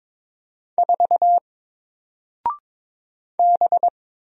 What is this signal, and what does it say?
Morse code, heard as a keyed tone, about 22 wpm, 700 Hz. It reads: 4 B
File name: Single-Letter-Number-22wpm